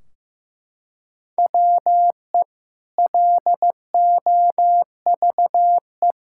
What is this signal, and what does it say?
Morse code, heard as a keyed tone, about 15 wpm, 700 Hz. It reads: WE LOVE
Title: Sentences from Top 1600 Words 15wpm